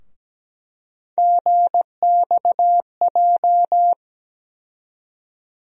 Morse code, heard as a keyed tone, about 17 wpm, 700 Hz. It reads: GXJ
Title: USA Call Sign Postfixes - 3 Letters 17wpm